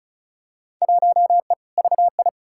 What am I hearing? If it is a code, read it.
Morse code, heard as a keyed tone, about 36 wpm, 700 Hz. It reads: 1E VI
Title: Field Day Exchanges - 35wpm